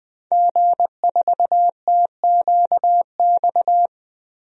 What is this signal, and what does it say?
Morse code, heard as a keyed tone, about 20 wpm, 700 Hz. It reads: G4TQX
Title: DXCC-100 Call Signs - 5 Characters - 20wpm